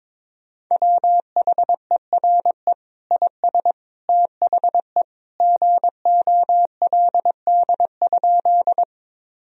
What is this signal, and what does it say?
Morse code, heard as a keyed tone, about 22 wpm, 700 Hz. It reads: WHERE IS THE GOLD?
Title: Sentences from Top 1000 Words - Review Part 3 - 22wpm